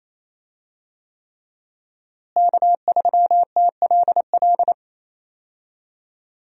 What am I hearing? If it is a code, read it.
Morse code, heard as a keyed tone, about 28 wpm, 700 Hz. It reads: K3TLL
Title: USA Call Signs - 5 Characters - Repeated 3x - 28wpm